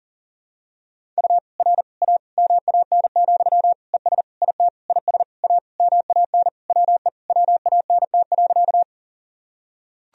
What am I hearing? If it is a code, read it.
Morse code, heard as a keyed tone, about 40 wpm, 700 Hz. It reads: U R A MAN, ES IT IS A MAN WE WANT.
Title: Sentences from Top 200 Words and Abbreviations 40wpm